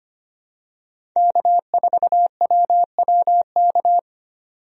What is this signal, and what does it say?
Morse code, heard as a keyed tone, about 25 wpm, 700 Hz. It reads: K4WWK
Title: USA Call Signs - 5 Characters - 25wpm